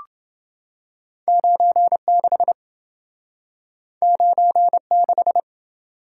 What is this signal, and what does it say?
Morse code, heard as a keyed tone, about 28 wpm, 700 Hz. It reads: 96 96